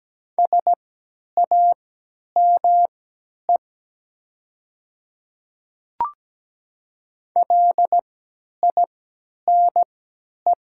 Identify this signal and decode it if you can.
Morse code, heard as a keyed tone, about 17 wpm, spaced as for 10 wpm, 700 Hz. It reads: SAME LINE
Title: 3x Character Spacing - Top 500 Words - 17wpm